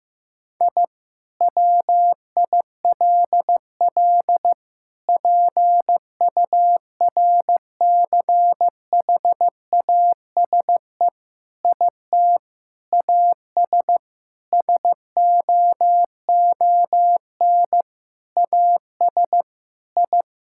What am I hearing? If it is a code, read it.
Morse code, heard as a keyed tone, about 15 wpm, 700 Hz. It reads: I WILL PURCHASE IT AS SOON AS I
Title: Sentences from Top 1000 Words - Review Part 5 - 15wpm